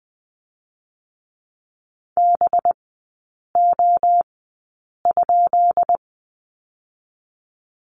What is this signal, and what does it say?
Morse code, heard as a keyed tone, about 20 wpm, spaced as for 8 wpm, 700 Hz. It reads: BO?
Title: 184 Lesson 30e - Letter b - Three Characters 20wpm